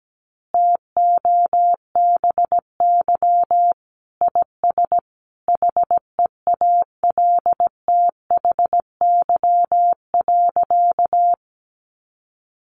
Morse code, heard as a keyed tone, about 17 wpm, 700 Hz. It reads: TOBY IS HEALTHY.
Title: Easy Phrases 17wpm